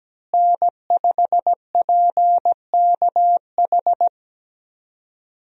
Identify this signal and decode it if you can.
Morse code, heard as a keyed tone, about 17 wpm, 700 Hz. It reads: N5PKH
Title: USA Call Signs - 5 Characters - Repeated 2x - 17wpm